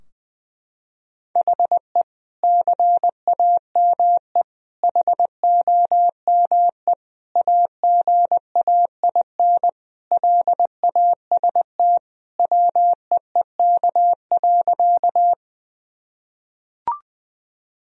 Morse code, heard as a keyed tone, about 21 wpm, 700 Hz. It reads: HE CAME HOME AGAIN LAST WEEK.
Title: Sentences from Top 1000 Words - Review Part 2 - 20wpm